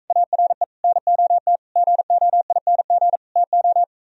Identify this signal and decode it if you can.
Morse code, heard as a keyed tone, about 45 wpm, 700 Hz. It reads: ARE NOT GOING TO